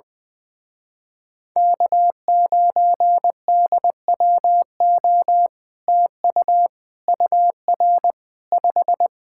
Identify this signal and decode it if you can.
Morse code, heard as a keyed tone, about 22 wpm, 700 Hz. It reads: K9DWO TU UR 5
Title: Speed-Racing Common QSO Phrases - 20wpm